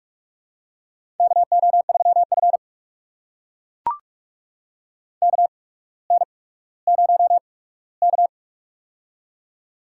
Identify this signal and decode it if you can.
Morse code, heard as a keyed tone, about 44 wpm, spaced as for 19 wpm, 700 Hz. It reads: KO3F KN0K
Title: USA Call Signs - 4 Characters - 8x Character Spacing - 45wpm